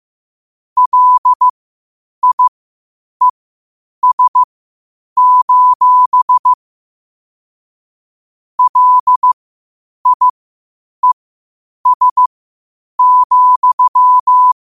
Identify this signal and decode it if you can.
Morse code, heard as a keyed tone, about 15 wpm, spaced as for 8 wpm, 1000 Hz. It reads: LIES: LIES,
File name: qotdfiles / 8wpm